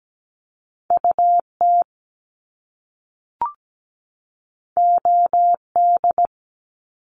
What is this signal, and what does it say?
Morse code, heard as a keyed tone, about 17 wpm, 700 Hz. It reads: UT OD
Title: Top Two Letter Combinations in Words 17wpm